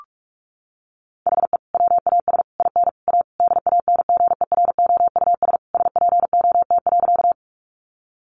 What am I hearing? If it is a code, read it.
Morse code, heard as a keyed tone, about 44 wpm, 700 Hz. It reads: HE WAS IN A DANGEROUS SPOT.
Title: Sentences from Top 1200 Words 45wpm